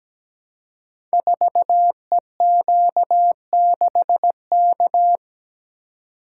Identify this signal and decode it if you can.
Morse code, heard as a keyed tone, about 17 wpm, 700 Hz. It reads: 4EQ6K
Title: Code Groups - 5 Letter:Numbers - 17wpm